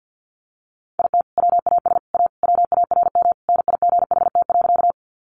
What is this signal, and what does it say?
Morse code, heard as a keyed tone, about 50 wpm, 700 Hz. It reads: IT WAS A WARM NIGHT.
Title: Sentences from Top 500 Words 50wpm